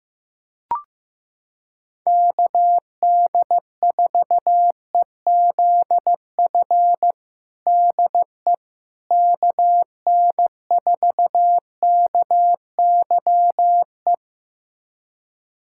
Morse code, heard as a keyed tone, about 15 wpm, 700 Hz. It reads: KD4EZF DE KN4KYE
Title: USA Callsign DE Callsign 15wpm